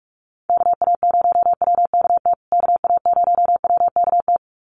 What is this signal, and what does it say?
Morse code, heard as a keyed tone, about 44 wpm, 700 Hz. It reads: KA0WKT KA0WKT
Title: USA Call Signs Repeated 2x 45wpm